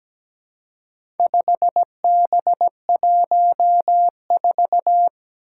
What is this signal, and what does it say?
Morse code, heard as a keyed tone, about 17 wpm, 700 Hz. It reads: 5B14
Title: Four Challenging Letter-Number Combinations 17wpm